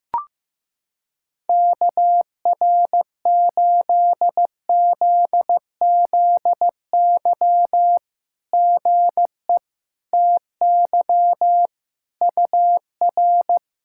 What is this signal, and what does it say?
Morse code, heard as a keyed tone, about 15 wpm, 700 Hz. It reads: KR8ZZY GE TY UR
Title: SOTA QSO Practice 15wpm